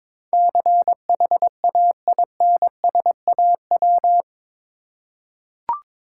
Morse code, heard as a keyed tone, about 22 wpm, 690 Hz. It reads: CHAINSAW
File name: Difficult Words with Hints - 22wpm